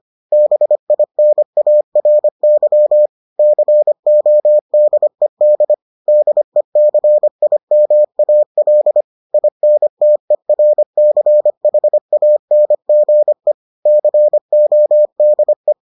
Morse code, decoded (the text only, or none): BINARY CODED DECIMAL INTERCHANGE CODE